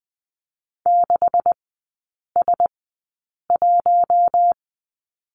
Morse code, heard as a keyed tone, about 20 wpm, spaced as for 8 wpm, 700 Hz. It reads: 6S1
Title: 152 Lesson 25e - Letter p - Three Characters 20wpm